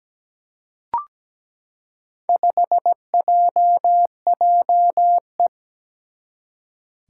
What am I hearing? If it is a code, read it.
Morse code, heard as a keyed tone, about 17 wpm, 700 Hz. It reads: 5JJE
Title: Four Challenging Letter-Number Combinations 17wpm